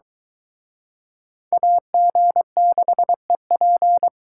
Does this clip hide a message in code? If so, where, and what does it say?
Yes: Morse code from 1.5 to 4.1 s, heard as a keyed tone, about 24 wpm, 700 Hz. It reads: AG6EP